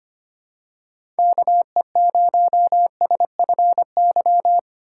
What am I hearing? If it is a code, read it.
Morse code, heard as a keyed tone, about 25 wpm, 700 Hz. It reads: KE0SFY
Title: USA Call Signs - 6 Characters - 4x Character Spacing - 25wpm